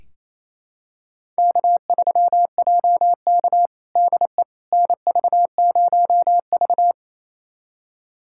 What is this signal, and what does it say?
Morse code, heard as a keyed tone, about 28 wpm, 690 Hz. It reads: K3JK DE NV0V
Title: USA Callsign DE Callsign - 4 Characters - 28wpm